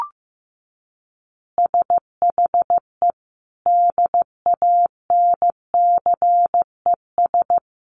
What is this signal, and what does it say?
Morse code, heard as a keyed tone, about 15 wpm, 700 Hz. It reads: SHE DANCES
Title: Easy Phrases 15wpm